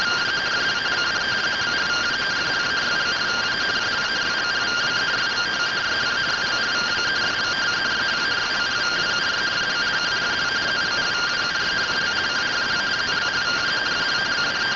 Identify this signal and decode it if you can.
Two-tone teleprinter signal, about 170 Hz shift.